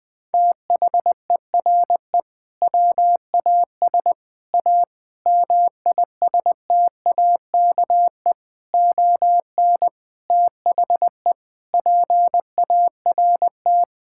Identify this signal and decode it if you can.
Morse code, heard as a keyed tone, about 21 wpm, 700 Hz. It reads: THERE WAS A MISTAKE ON THE PART
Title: Sentences with 5-Word Phrases - Spoken Beginning - 20wpm